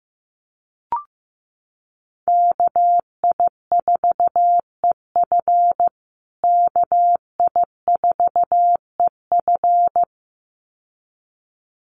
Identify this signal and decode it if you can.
Morse code, heard as a keyed tone, about 15 wpm, 700 Hz. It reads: KI4EF KI4EF
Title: USA Call Signs Repeated 2x 15wpm